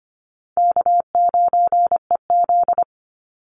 Morse code, heard as a keyed tone, about 25 wpm, 700 Hz. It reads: K9EZ
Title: NA SOTA Chaser Callsigns 25wpm